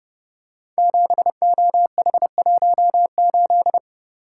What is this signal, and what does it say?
Morse code, heard as a keyed tone, about 30 wpm, 700 Hz. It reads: 7OH18
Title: Code Groups - 5 Letter:Numbers - 30wpm